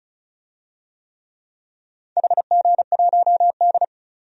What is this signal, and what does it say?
Morse code, heard as a keyed tone, about 36 wpm, 700 Hz. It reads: HG1D
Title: DXCC-100 Call Signs - 4 Characters - 35wpm